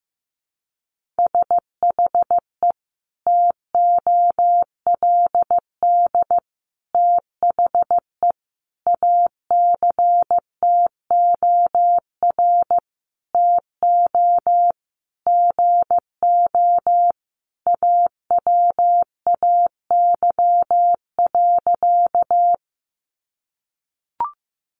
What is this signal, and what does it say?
Morse code, heard as a keyed tone, about 15 wpm, 700 Hz. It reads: SHE TOLD THE ACTOR TO GO AWAY.
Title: Sentences from Top 1500 Words 15wpm